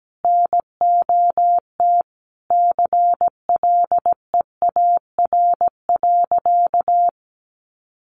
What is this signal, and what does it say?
Morse code, heard as a keyed tone, about 17 wpm, 700 Hz. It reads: NOT CLEAR.
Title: Sentences from Top 1300 Words 17wpm